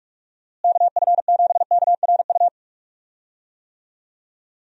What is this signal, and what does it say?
Morse code, heard as a keyed tone, about 46 wpm, 700 Hz. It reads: KF7KRU
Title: USA Call Signs - 6 Characters - Repeated 3x - 45wpm